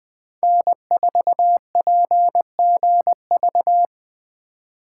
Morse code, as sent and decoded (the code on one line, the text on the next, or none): -. ....- .--. --. ...-
N4PGV